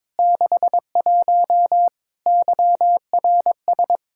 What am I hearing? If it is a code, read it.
Morse code, heard as a keyed tone, about 22 wpm, 700 Hz. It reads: 61 YRS
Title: K1USN SST QSO Practice 22wpm